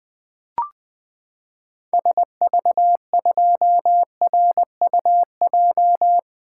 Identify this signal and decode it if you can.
Morse code, heard as a keyed tone, about 20 wpm, 700 Hz. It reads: SV2RUJ
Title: 2023 SOTA Activator Call Signs - 20wpm